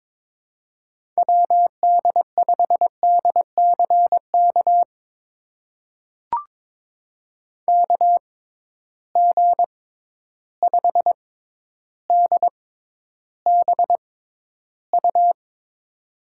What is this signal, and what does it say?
Morse code, heard as a keyed tone, about 22 wpm, spaced as for 11 wpm, 700 Hz. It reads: WD5DCK KG5DBU
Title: USA Call Signs - 6 Characters - 6x Character Spacing - 22wpm